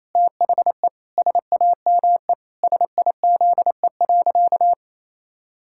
Morse code, heard as a keyed tone, about 28 wpm, 700 Hz. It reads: THE SAME SIZE.
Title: Pangram Challenge v2 - 28wpm